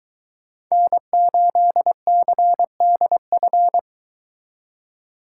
Morse code, heard as a keyed tone, about 22 wpm, 700 Hz. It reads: N8CDF